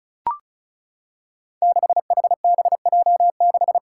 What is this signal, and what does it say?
Morse code, heard as a keyed tone, about 36 wpm, 700 Hz. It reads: 6HBJ6
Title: Five Challenging Letter-Number Combinations 35wpm